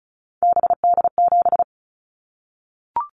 Morse code, heard as a keyed tone, about 36 wpm, 700 Hz. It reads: BD7
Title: Callsign Prefixes for Top 100 DXCC - 35wpm